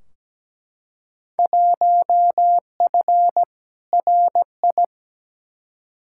Morse code, heard as a keyed tone, about 17 wpm, 700 Hz. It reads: 1F RI
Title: Field Day Exchanges Encore - 17wpm